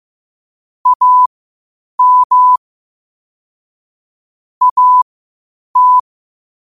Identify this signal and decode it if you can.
Morse code, heard as a keyed tone, about 15 wpm, spaced as for 8 wpm, 1000 Hz. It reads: AM AT
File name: qotdfiles / 8wpm